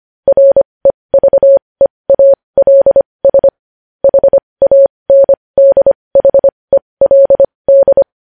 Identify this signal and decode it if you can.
Morse code, heard as a keyed tone, about 25 wpm, spaced as for 20 wpm, 550 Hz. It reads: REVEALS HANDHELD
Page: news-headlines-in-morse-code-at-20-wpm